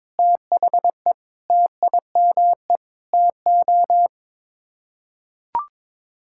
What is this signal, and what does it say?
Morse code, heard as a keyed tone, about 22 wpm, 700 Hz. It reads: THE TIME TO
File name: Sets of 9 Words - 22wpm